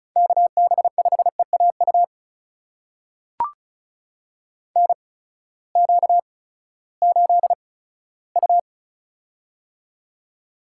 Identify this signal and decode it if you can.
Morse code, heard as a keyed tone, about 36 wpm, spaced as for 18 wpm, 700 Hz. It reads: KB5EAU NQ8U
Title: USA Call Signs - 8x Character Spacing - 35wpm